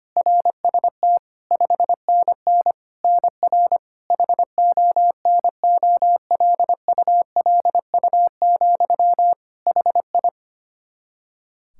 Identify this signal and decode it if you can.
Morse code, heard as a keyed tone, about 25 wpm, 700 Hz. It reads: RST 5NN NR HONOLULU, HI
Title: Practice QSO - Exchange 1 - Beginning - 25wpm